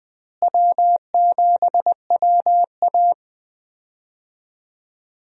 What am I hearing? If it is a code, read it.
Morse code, heard as a keyed tone, about 20 wpm, 700 Hz. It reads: W7WA
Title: USA Call Signs - 4 Characters - Repeated 3x - 20wpm